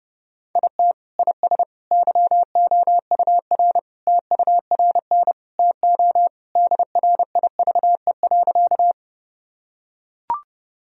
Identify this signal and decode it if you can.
Morse code, heard as a keyed tone, about 32 wpm, 700 Hz. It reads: IT IS YOUR TURN TO DRIVE.
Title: Sentences from Top 1000 Words - Review Part 2 - 30wpm